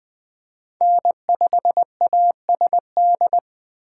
Morse code, heard as a keyed tone, about 21 wpm, 700 Hz. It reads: N5ASD